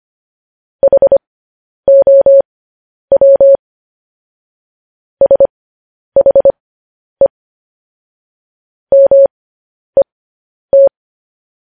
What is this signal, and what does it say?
Morse code, heard as a keyed tone, about 25 wpm, spaced as for 10 wpm, 550 Hz. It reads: HOW SHE MET